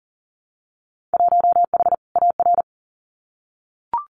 Morse code, heard as a keyed tone, about 39 wpm, 700 Hz. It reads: 1H AR
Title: Winter Field Day Exchanges 40wpm